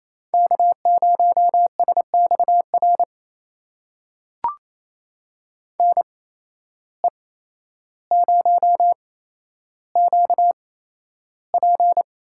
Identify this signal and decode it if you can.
Morse code, heard as a keyed tone, about 28 wpm, spaced as for 12 wpm, 700 Hz. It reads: K0SXR NE0QP
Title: USA Call Signs - 8x Character Spacing - 28wpm